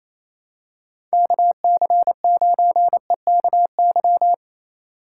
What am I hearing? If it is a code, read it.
Morse code, heard as a keyed tone, about 28 wpm, 700 Hz. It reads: KC9EKY
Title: USA Call Signs - 6 Characters - 4x Character Spacing - 28wpm